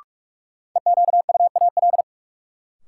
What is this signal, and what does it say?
Morse code, heard as a keyed tone, about 42 wpm, 700 Hz. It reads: EQUAL
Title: Sentences from Top 1400 Words - New Word Drill - 45wpm